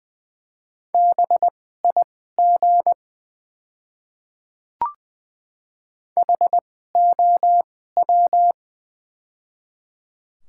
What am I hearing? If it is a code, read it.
Morse code, heard as a keyed tone, about 21 wpm, spaced as for 15 wpm, 700 Hz. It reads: BIG HOW
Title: 2x Character Spacing - Top 200 Words - 20wpm